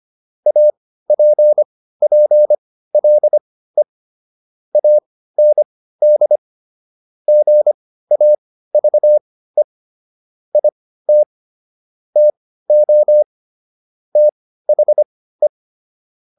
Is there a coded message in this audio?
Yes: Morse code from 0.5 to 15.5 s, heard as a keyed tone, about 25 wpm, spaced as for 15 wpm, 600 Hz. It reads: APPLE AND GAVE IT TO THE